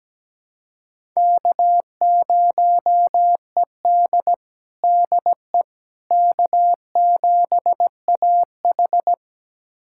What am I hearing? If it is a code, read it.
Morse code, heard as a keyed tone, about 17 wpm, 700 Hz. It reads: K0ED DE K7AH
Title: USA Callsign DE Callsign - 4 Characters - 17wpm